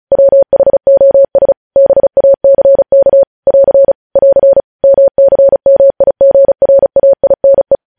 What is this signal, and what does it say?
Morse code, heard as a keyed tone, about 36 wpm, 550 Hz. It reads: WHOS BACK <AR> <AR> MCMIGRAINE
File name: podcast / News headlines-1x-35x35